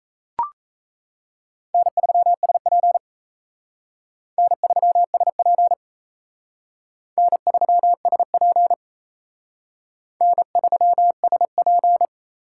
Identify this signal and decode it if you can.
Morse code, heard as a keyed tone, about 35 wpm, 700 Hz. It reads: N3SP N3SP N3SP N3SP